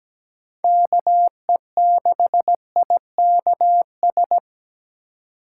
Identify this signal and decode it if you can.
Morse code, heard as a keyed tone, about 20 wpm, 700 Hz. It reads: KE6IKS
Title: Speed Racing - USA 2x3 Call Signs 17wpm